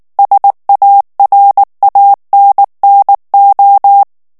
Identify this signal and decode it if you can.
Morse code, heard as a keyed tone, about 19 wpm, 800 Hz. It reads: SARANNO
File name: morse mp3 / ITALIEN texte clair 2 vitesse 18 mots minute